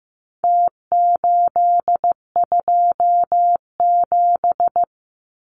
Morse code, heard as a keyed tone, about 15 wpm, 700 Hz. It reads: T827
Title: Numbers - Contest Style - 4-Digit 15wpm